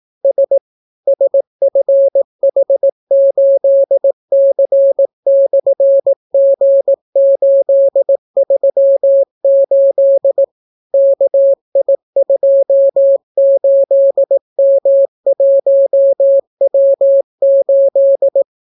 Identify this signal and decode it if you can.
Morse code, heard as a keyed tone, about 18 wpm, 550 Hz. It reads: S SFH8C/G838 KI28M1W8